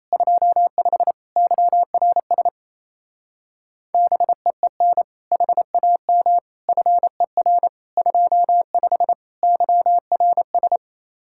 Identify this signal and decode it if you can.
Morse code, heard as a keyed tone, about 33 wpm, 700 Hz. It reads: 25 YRS BEEN HAM FER 25 YRS
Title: Speed-Racing Common QSO Phrases - 28wpm